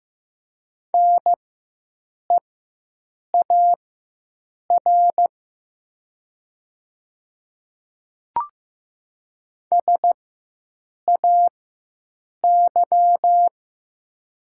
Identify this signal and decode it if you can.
Morse code, heard as a keyed tone, about 15 wpm, spaced as for 7 wpm, 700 Hz. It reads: NEAR SAY
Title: 4x Character Spacing - Top 200 Words - 15wpm